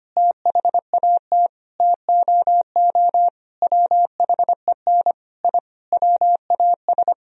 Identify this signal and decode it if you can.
Morse code, heard as a keyed tone, about 25 wpm, 700 Hz. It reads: THAT TOO WHEN I WAS
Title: Sentences from Top 1000 Words - Review Part 3 - 25wpm